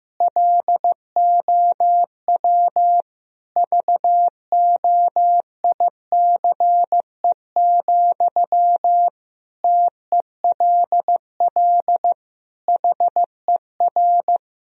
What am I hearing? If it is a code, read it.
Morse code, heard as a keyed tone, about 15 wpm, 700 Hz. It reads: LOW VOICE, TELL HER
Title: Sentences from Top 1000 Words - Review Part 1 - 15wpm